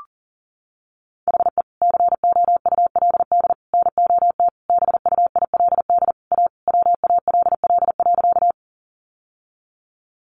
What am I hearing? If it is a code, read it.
Morse code, heard as a keyed tone, about 37 wpm, 700 Hz. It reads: HE COULD NOT BUILD A WALL.
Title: Sentences from Top 1100 Words 40wpm